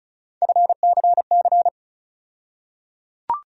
Morse code, heard as a keyed tone, about 34 wpm, 700 Hz. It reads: FCC
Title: USA Call Sign Postfixes - 3 Letters 35wpm